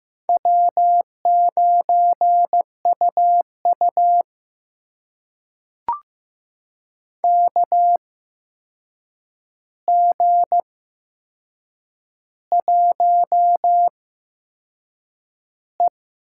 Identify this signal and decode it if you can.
Morse code, heard as a keyed tone, about 15 wpm, spaced as for 6 wpm, 700 Hz. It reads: W9UU KG1E
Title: USA Call Signs - 4 Characters - 8x Character Spacing - 15wpm